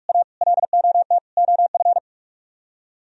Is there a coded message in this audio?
Yes: Morse code from 0.1 to 2.0 s, heard as a keyed tone, about 48 wpm, 700 Hz. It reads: A LOT OF